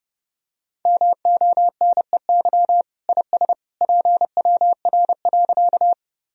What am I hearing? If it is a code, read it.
Morse code, heard as a keyed tone, about 30 wpm, 700 Hz. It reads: MONEY IS PWR.
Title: Sentences from Top 500 Words and Abbreviations 30wpm